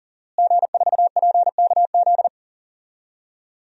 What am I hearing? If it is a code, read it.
Morse code, heard as a keyed tone, about 40 wpm, 700 Hz. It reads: G4PKZ
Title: DXCC-100 Call Signs - 5 Characters - Repeated 3x - 40wpm